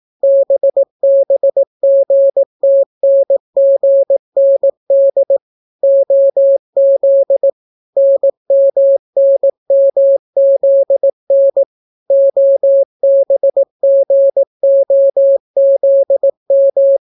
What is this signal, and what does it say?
Morse code, heard as a keyed tone, about 18 wpm, 550 Hz. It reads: BBGTNGND OZ NMNMZN OBGOZM